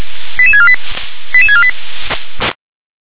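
Five-tone selective call (ZVEI), heard as quick runs of five tones, tones of about 70 ms. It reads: 88648 88648
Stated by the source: ZVEI five-tone call 88646 received by SDR.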